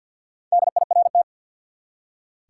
Morse code, heard as a keyed tone, about 46 wpm, 700 Hz. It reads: DIRT
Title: Four Letter Words - 50wpm